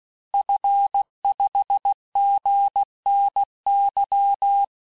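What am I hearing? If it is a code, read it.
Morse code, heard as a keyed tone, about 16 wpm, 800 Hz. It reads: F5GNY
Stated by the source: Morse course text, end of QSO, sent at 15 words per minute.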